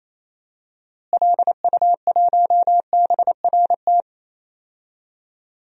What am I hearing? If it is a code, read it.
Morse code, heard as a keyed tone, about 27 wpm, 700 Hz. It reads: LU1BRT
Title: DXCC-100 Call Signs - 6 Characters - Repeated 3x - 28wpm